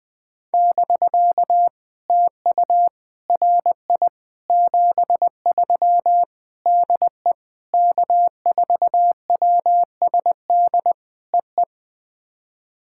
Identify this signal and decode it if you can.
Morse code, heard as a keyed tone, about 20 wpm, 700 Hz. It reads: <BK> TU RI 73 DE K4WSD EE
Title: POTA QSO Practice - 20wpm